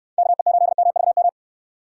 Morse code, heard as a keyed tone, about 66 wpm, 700 Hz. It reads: KE8MVM